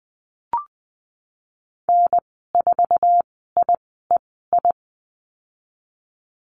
Morse code, heard as a keyed tone, about 20 wpm, spaced as for 15 wpm, 700 Hz. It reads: N4IEI
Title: 088 Lesson 15g - Number 5 - Call Signs 20wpm